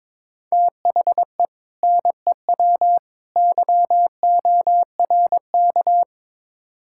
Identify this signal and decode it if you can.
Morse code, heard as a keyed tone, about 22 wpm, 700 Hz. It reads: THE NEW YORK
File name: practice / 22wpm / Build A Phrase - 5-Word Phrases - 22wpm